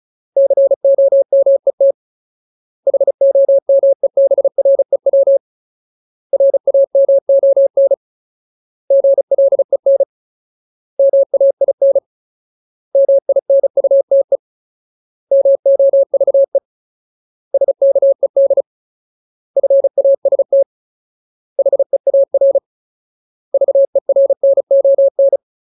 Morse code, heard as a keyed tone, about 34 wpm, 550 Hz. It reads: COMET HOMEBREW RAMON GLEN MAIN MINUTE MOVE SKED FAST HEAR VERNON
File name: Ultimate Word List / 35WPM Ultimate